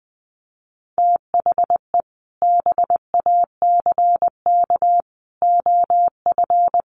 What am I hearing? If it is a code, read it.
Morse code, heard as a keyed tone, about 20 wpm, 700 Hz. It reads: THE BACK OF
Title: Sets of 5 Words Encore2 20wpm